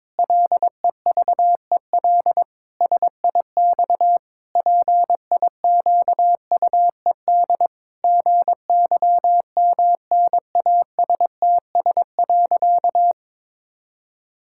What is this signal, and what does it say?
Morse code, heard as a keyed tone, about 22 wpm, 700 Hz. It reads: LEVEL SIX PIQUED GYMNASTS.
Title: Pangram Challenge v2 - 22wpm